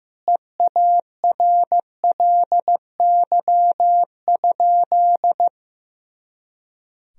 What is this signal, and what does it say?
Morse code, heard as a keyed tone, about 15 wpm, 700 Hz. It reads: EARLY?
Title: Sentences from Top 400 Words and Abbreviations 15wpm